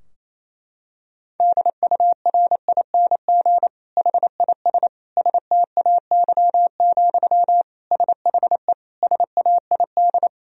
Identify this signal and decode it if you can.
Morse code, heard as a keyed tone, about 28 wpm, 700 Hz. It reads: DURING HIS STAY, SHE SAID
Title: Sentences from Top 1000 Words - Review Part 2 - 28wpm